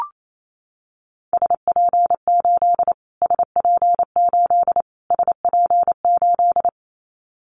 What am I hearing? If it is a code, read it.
Morse code, heard as a keyed tone, about 28 wpm, 700 Hz. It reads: SP8 SP8 SP8
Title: Callsign Prefixes for Top 25 DXCC - Repeated 3x - 28wpm